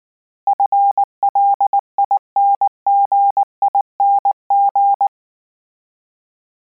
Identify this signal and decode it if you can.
Morse code, heard as a keyed tone, about 19 wpm, 800 Hz. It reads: FLINGING